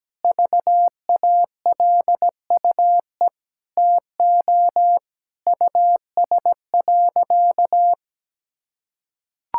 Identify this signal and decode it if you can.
Morse code, heard as a keyed tone, about 17 wpm, 700 Hz. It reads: VALUE TO US.
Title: Sentences from Top 1000 Words - Review Part 5 - 17wpm